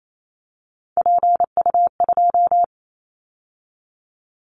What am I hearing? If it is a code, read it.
Morse code, heard as a keyed tone, about 28 wpm, 700 Hz. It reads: PU2
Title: Callsign Prefixes for Top 50 DXCC - 28wpm